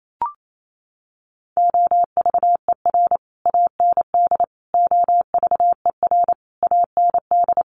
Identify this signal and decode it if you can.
Morse code, heard as a keyed tone, about 28 wpm, 700 Hz. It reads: OVER AND OVER AND
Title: Sets of 5 Words Encore 28wpm